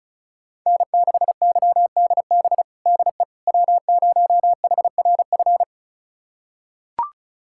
Morse code, heard as a keyed tone, about 36 wpm, 700 Hz. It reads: N6YDB DE W0HRF
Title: USA Callsign DE Callsign - 5 Characters - 35wpm